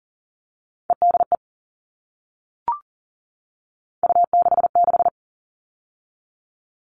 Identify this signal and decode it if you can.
Morse code, heard as a keyed tone, about 40 wpm, 700 Hz. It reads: EDE U66
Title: All Cut Numbers - 3-Digit 40wpm